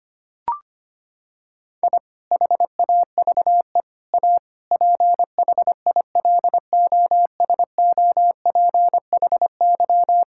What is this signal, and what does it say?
Morse code, heard as a keyed tone, about 25 wpm, 700 Hz. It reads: I HAVE A PHILOSOPHY
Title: Sentences from Top 1000 Words - Review Part 6 - 25wpm